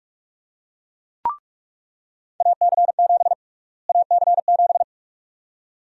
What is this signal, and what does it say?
Morse code, heard as a keyed tone, about 46 wpm, 700 Hz. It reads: AC7 AC7
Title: USA Call Sign Prefixes - Repeated 2x - 45wpm